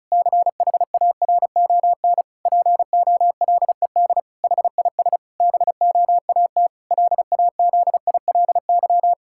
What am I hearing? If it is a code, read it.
Morse code, heard as a keyed tone, about 36 wpm, 700 Hz. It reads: CHARON POLED HIS BOAT LAZILY
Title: Pangram Challenge v2 - 35wpm